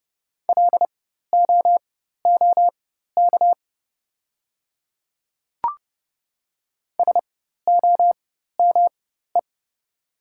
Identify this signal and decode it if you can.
Morse code, heard as a keyed tone, about 30 wpm, spaced as for 14 wpm, 700 Hz. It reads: LOOK SOME
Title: 4x Character Spacing - Top 100 Words - 30wpm